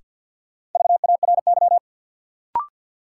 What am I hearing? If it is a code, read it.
Morse code, heard as a keyed tone, about 52 wpm, 700 Hz. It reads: VARY
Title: Sentences from Top 1200 Words - New Word Drill - 50wpm